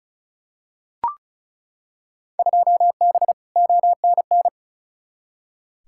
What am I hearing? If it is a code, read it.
Morse code, heard as a keyed tone, about 36 wpm, 700 Hz. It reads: 2B ONN
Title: Field Day Exchanges - 35wpm